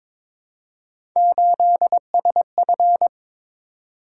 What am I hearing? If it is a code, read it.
Morse code, heard as a keyed tone, about 22 wpm, 700 Hz. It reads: 8SF
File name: practice / 22wpm / Three Letter-Number Combinations - 22wpm